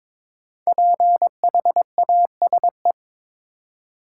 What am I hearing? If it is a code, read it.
Morse code, heard as a keyed tone, about 22 wpm, 700 Hz. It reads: PHASE
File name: 5 Letter Words - First 3 Letters Given - 22wpm